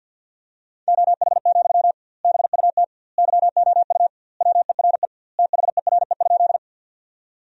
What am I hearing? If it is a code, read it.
Morse code, heard as a keyed tone, about 52 wpm, 700 Hz. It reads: OH, BUT YOU WERE THERE?
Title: 2x Word-Spacing - Sentences from Top 500 Words - 50wpm